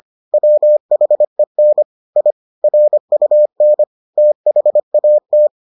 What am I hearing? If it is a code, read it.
Morse code, heard as a keyed tone, about 25 wpm, 600 Hz. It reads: WHEN I RUN THAT